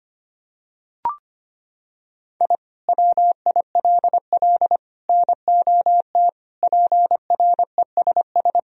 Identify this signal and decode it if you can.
Morse code, heard as a keyed tone, about 25 wpm, 700 Hz. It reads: I WILL NOT PRESS
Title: Sentences from Top 1000 Words - Review Part 1 - 25wpm